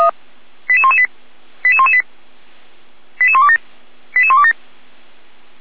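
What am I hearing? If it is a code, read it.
Five-tone selective call (ZVEI), heard as quick runs of five tones, tones of about 70 ms. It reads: 88118 88118 88127 88127
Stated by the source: ZVEI five-tone call received by SDR, archived by feuerwehr-orlamuende.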